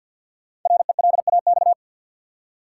Morse code, heard as a keyed tone, about 46 wpm, 700 Hz. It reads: RELAX
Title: Five Letter Words - 50wpm